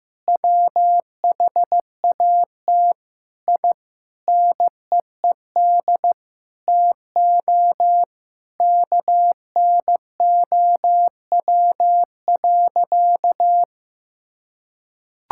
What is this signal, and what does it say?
Morse code, heard as a keyed tone, about 16 wpm, 700 Hz. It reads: WHAT I NEED TO KNOW.
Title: Sentences from Top 1000 Words - Review Part 5 - 15wpm